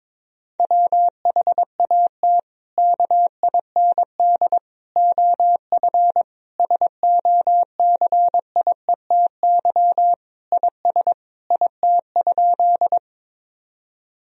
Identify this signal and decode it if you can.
Morse code, heard as a keyed tone, about 21 wpm, 700 Hz. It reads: WHAT KIND OF SOCIETY IS IT?
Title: Sentences from Top 1000 Words - Review Part 4 - 22wpm